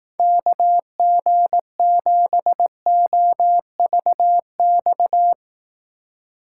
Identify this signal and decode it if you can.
Morse code, heard as a keyed tone, about 20 wpm, 700 Hz. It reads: KG7OVX